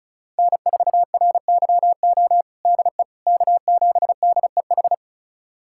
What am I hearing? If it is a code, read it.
Morse code, heard as a keyed tone, about 34 wpm, 700 Hz. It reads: N4RYO DE K7DEH
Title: USA Callsign DE Callsign - 5 Characters - 35wpm